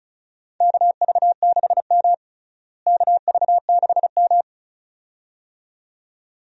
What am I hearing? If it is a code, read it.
Morse code, heard as a keyed tone, about 36 wpm, 700 Hz. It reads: KV6M KV6M
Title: USA Call Signs - 4 Characters - Repeated 2x - 35wpm